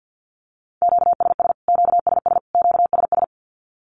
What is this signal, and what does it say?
Morse code, heard as a keyed tone, about 48 wpm, 700 Hz. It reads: QSS QSS QSS
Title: Q-Codes Repeated 3x 50wpm